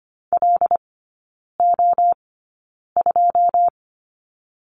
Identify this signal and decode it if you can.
Morse code, heard as a keyed tone, about 25 wpm, 700 Hz. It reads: L O 2
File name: Lesson 17e - Letter u - Three Characters 25wpm